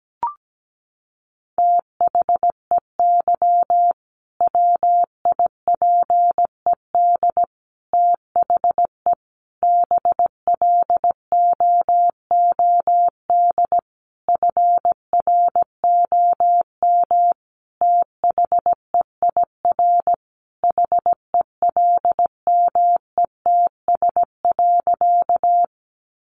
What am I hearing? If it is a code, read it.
Morse code, heard as a keyed tone, about 17 wpm, 700 Hz. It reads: THEY WIPED THE BLOOD FROM THEIR HELMETS.
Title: A Sound Of Thunder 17wpm